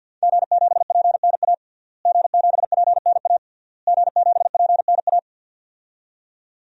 Morse code, heard as a keyed tone, about 52 wpm, 700 Hz. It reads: G7PNA G7PNA G7PNA
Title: DXCC-100 Call Signs - 5 Characters - Repeated 3x - 50wpm